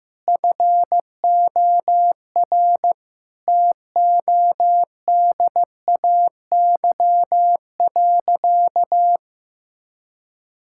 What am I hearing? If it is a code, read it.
Morse code, heard as a keyed tone, about 15 wpm, 700 Hz. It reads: FOR TODAY.
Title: Sentences from Top 1000 Words - Review Part 5 - 15wpm